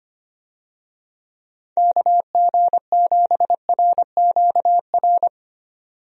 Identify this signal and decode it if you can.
Morse code, heard as a keyed tone, about 25 wpm, 700 Hz. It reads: KG7RQR
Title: USA Call Signs - 6 Characters - 25wpm